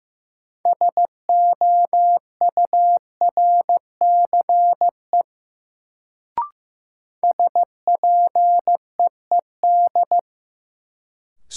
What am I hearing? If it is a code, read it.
Morse code, heard as a keyed tone, about 15 wpm, 700 Hz. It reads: SOURCE SPEED
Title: Sentences from Top 800 Words - New Word Drill - 15wpm